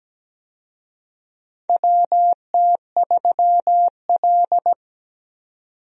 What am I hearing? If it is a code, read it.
Morse code, heard as a keyed tone, about 17 wpm, 700 Hz. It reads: WT3L
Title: USA Call Signs - 4 Characters - 17wpm